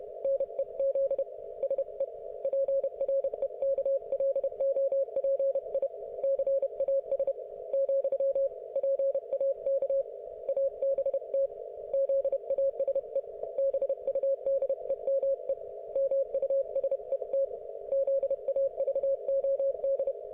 Morse code, heard as a keyed tone, about 31 wpm, 530 Hz. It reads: NEZ SE PREKLOPI CAS , PAK ABT ZASE BUDEME MUSET ZAVOD